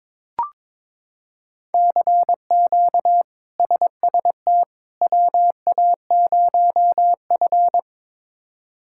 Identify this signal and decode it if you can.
Morse code, heard as a keyed tone, about 22 wpm, 700 Hz. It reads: CQ SST WA0F
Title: K1USN SST QSO Practice - 22wpm